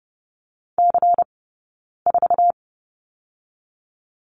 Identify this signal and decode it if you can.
Morse code, heard as a keyed tone, about 30 wpm, 700 Hz. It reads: C 4
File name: Lesson 18d - Letter m - Two Characters 30wpm